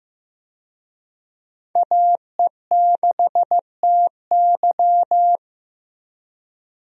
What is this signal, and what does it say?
Morse code, heard as a keyed tone, about 15 wpm, 700 Hz. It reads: AE6TY
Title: USA Call Signs - 5 Characters - Repeated 2x - 15wpm